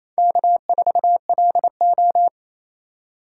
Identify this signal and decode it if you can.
Morse code, heard as a keyed tone, about 28 wpm, 700 Hz. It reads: K4LO
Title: USA Call Signs - 2x Character Spacing - 28wpm